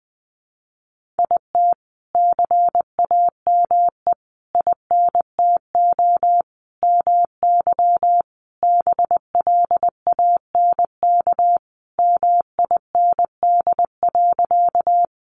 Morse code, heard as a keyed tone, about 20 wpm, 700 Hz. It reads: IT CAME INTO MY BLANK MIND.
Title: Sentences from Top 1500 Words 20wpm